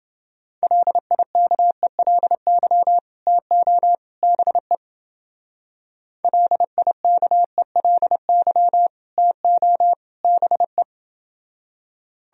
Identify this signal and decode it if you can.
Morse code, heard as a keyed tone, about 29 wpm, 700 Hz. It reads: LIKELY TO BE LIKELY TO BE